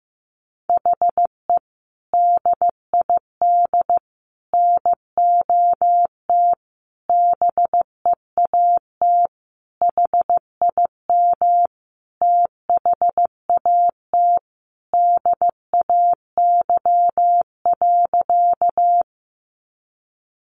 Morse code, heard as a keyed tone, about 15 wpm, 700 Hz. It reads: HE DID NOT BEAT HIM THAT DAY.
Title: Sentences from Top 1300 Words 15wpm